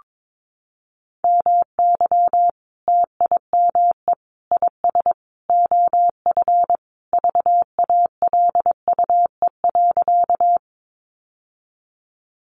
Morse code, heard as a keyed tone, about 22 wpm, 700 Hz. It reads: MY TIME IS OF VALUE.
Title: Sentences from Top 600 Words 22wpm